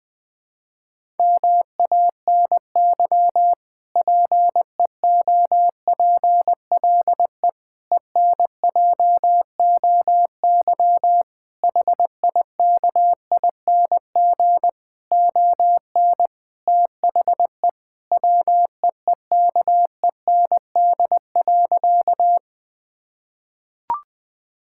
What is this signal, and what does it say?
Morse code, heard as a keyed tone, about 20 wpm, 700 Hz. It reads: MANY PEOPLE ENJOY HIKING ON THE WEEKEND.
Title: Difficult Sentences with Hints - 20wpm